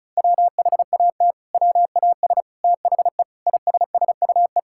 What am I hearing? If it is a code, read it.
Morse code, heard as a keyed tone, about 36 wpm, 700 Hz. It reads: WHAT WAS THE ISSUE
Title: Sentences from Top 1000 Words - Review Part 6 - 35wpm